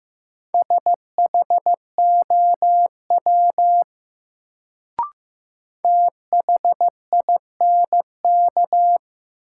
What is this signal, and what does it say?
Morse code, heard as a keyed tone, about 15 wpm, 700 Hz. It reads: SHOW THINK
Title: Sentences from Top 200 Words - New Word Drill - 15wpm